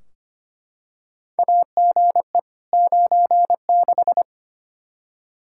Morse code, heard as a keyed tone, about 25 wpm, 700 Hz. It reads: AGE 96
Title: Practice QSO - Exchange 3 - More Info - 25wpm